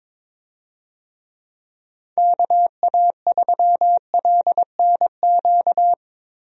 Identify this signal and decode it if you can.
Morse code, heard as a keyed tone, about 22 wpm, 700 Hz. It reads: KA3LNQ
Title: USA Call Signs - 6 Characters - Repeated 2x - 22wpm